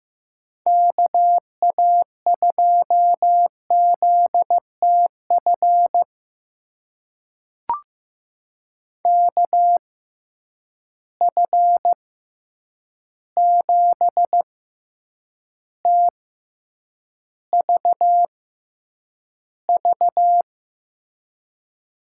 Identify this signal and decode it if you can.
Morse code, heard as a keyed tone, about 15 wpm, spaced as for 8 wpm, 700 Hz. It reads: KA2ZTF KF7TVV
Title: USA Call Signs - 6 Characters - 6x Character Spacing - 15wpm